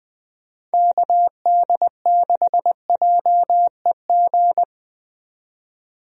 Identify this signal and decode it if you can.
Morse code, heard as a keyed tone, about 20 wpm, 700 Hz. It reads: KD6JEG
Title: USA Call Signs - 6 Characters - 20wpm